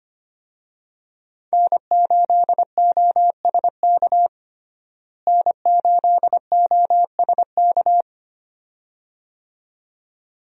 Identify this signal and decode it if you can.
Morse code, heard as a keyed tone, about 25 wpm, 700 Hz. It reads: N8OSK N8OSK
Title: USA Call Signs - 5 Characters - Repeated 2x - 25wpm